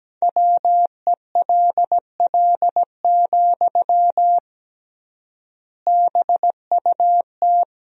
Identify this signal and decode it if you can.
Morse code, heard as a keyed tone, about 17 wpm, 700 Hz. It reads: WELL, BUT
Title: 3x Word-Spacing - Sentences from Top 500 Words - 17wpm